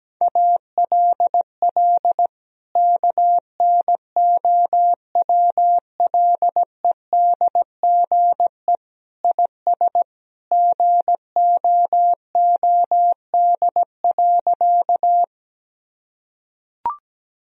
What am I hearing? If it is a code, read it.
Morse code, heard as a keyed tone, about 17 wpm, 700 Hz. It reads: ALL KNOWLEDGE IS GOOD.
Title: Sentences from Top 1000 Words - Review Part 3 - 17wpm